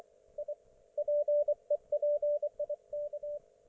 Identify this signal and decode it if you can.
Morse code, heard as a keyed tone, about 24 wpm, 590 Hz. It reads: I PEPIK
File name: Archiv / CUC212 / ok1fhi cul212